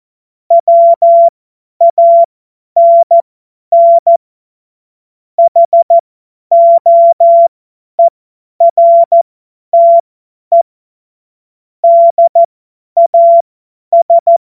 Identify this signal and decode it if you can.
Morse code, heard as a keyed tone, about 14 wpm, spaced as for 10 wpm, 680 Hz. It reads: WANN HOERTE DAS
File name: Morsekurs / Witze / Witz 30. August 2022 50 F70